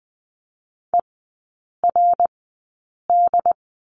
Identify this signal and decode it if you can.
Morse code, heard as a keyed tone, about 20 wpm, spaced as for 8 wpm, 700 Hz. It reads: ERD
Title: 067 Lesson 12e - Letter d - Three Characters 20wpm